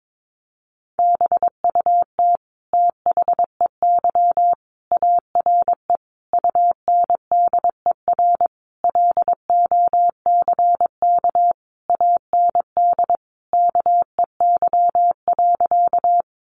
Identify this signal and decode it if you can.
Morse code, heard as a keyed tone, about 22 wpm, 700 Hz. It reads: BUT THEY ARE UNDER LOCK AND KEY.
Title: Sentences from Top 1300 Words 22wpm